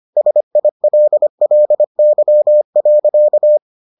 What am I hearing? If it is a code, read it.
Morse code, heard as a keyed tone, about 25 wpm, 600 Hz. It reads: SILLY.